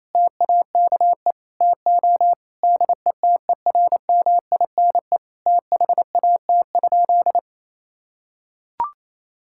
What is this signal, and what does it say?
Morse code, heard as a keyed tone, about 28 wpm, 700 Hz. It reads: TAKE TO DETERMINE THAT?
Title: Sentences from Top 1000 Words - Review Part 5 - 28wpm